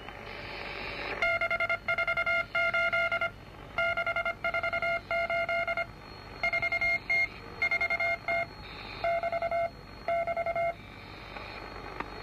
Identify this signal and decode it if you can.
Morse code, heard as a keyed tone, about 25 wpm, 2150 Hz, with noise. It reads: 648 648 4T 4T <BT> <BT>